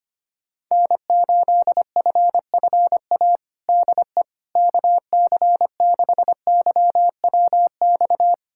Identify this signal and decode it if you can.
Morse code, heard as a keyed tone, about 25 wpm, 700 Hz. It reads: N8FFA DE KC6YWX
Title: USA Callsign DE Callsign 25wpm